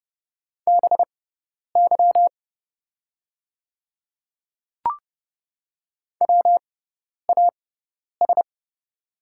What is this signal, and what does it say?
Morse code, heard as a keyed tone, about 30 wpm, spaced as for 10 wpm, 700 Hz. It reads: BY WAS
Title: 6x Character Spacing - Top 100 Words - 30wpm